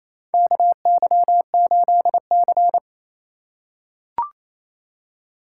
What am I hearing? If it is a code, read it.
Morse code, heard as a keyed tone, about 28 wpm, 700 Hz. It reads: KY8C
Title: USA Call Signs - 4 Characters - 2x Character Spacing - 28wpm